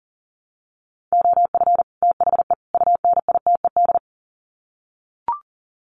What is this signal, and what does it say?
Morse code, heard as a keyed tone, about 40 wpm, 700 Hz. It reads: OF THE UNITED
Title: Sets of 3 Words 40wpm